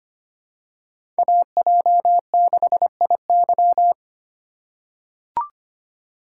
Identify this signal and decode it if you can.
Morse code, heard as a keyed tone, about 25 wpm, 700 Hz. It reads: AJ6IY
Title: USA Call Signs - 6x Character Spacing - 25wpm